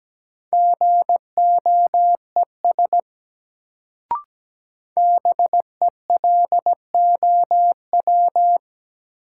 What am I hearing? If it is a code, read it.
Morse code, heard as a keyed tone, about 17 wpm, 700 Hz. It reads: GOES BELOW
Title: Sentences from Top 1000 Words - Review Part 5 - New Word Drill - 17wpm